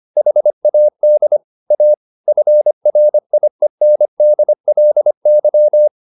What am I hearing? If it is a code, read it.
Morse code, heard as a keyed tone, about 25 wpm, 600 Hz. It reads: HAD A FRIENDLY
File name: the old man and the sea part 1 25wpm